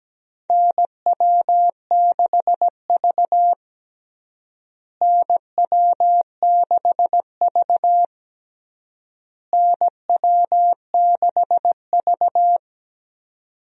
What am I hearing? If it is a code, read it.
Morse code, heard as a keyed tone, about 17 wpm, 700 Hz. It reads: NW6V NW6V NW6V
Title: USA Call Signs - 4 Characters - Repeated 3x - 17wpm